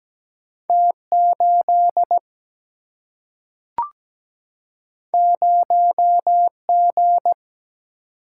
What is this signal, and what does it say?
Morse code, heard as a keyed tone, about 17 wpm, 700 Hz. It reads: T 8 0 G
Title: Single-Letter Single-Number 17wpm